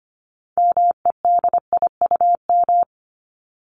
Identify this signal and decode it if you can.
Morse code, heard as a keyed tone, about 25 wpm, 700 Hz. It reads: MEDIUM
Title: Top 751-1000 Words 25wpm